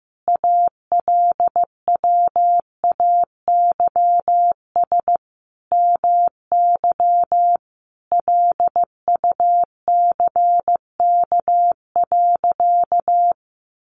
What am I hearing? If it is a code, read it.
Morse code, heard as a keyed tone, about 15 wpm, 700 Hz. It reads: ALWAYS MY LUCK.
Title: Sentences from Top 1500 Words 15wpm